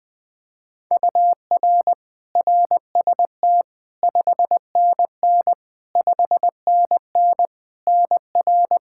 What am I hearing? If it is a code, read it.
Morse code, heard as a keyed tone, about 20 wpm, 700 Hz. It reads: UR RST 5NN 5NN NR
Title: K1USN SST QSO Practice 20wpm